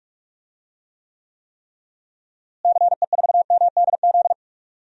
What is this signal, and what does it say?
Morse code, heard as a keyed tone, about 46 wpm, 700 Hz. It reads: CE4MDZ
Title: DXCC-100 Call Signs - 6 Characters - Repeated 3x - 45wpm